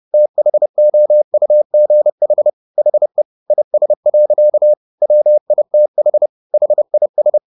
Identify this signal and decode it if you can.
Morse code, heard as a keyed tone, about 30 wpm, 600 Hz. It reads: THOUGH HE IS. WITH HIS
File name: the old man and the sea / the old man and the sea 1 30wpm